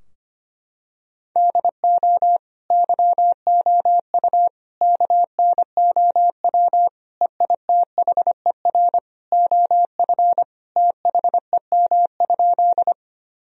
Morse code, heard as a keyed tone, about 25 wpm, 690 Hz. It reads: DO YOU KNOW EITHER OF THEM?
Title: Sentences from Top 1000 Words - Review Part 4 - 25wpm